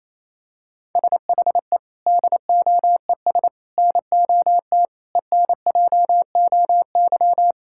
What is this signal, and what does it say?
Morse code, heard as a keyed tone, about 28 wpm, 700 Hz. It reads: SHE DOES NOT ENJOY
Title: Sentences from Top 1000 Words - Review Part 5 - 28wpm